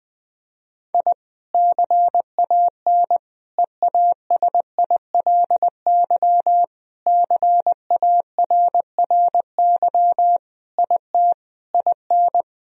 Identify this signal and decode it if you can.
Morse code, heard as a keyed tone, about 21 wpm, 700 Hz. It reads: I CAN EASILY CARRY IT IN
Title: Sentences from Top 1000 Words - Review Part 1 - 20wpm